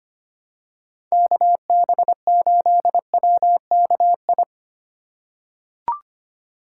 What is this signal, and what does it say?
Morse code, heard as a keyed tone, about 25 wpm, 700 Hz. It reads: KB8WKI
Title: USA Call Signs - 6 Characters - 2x Character Spacing - 25wpm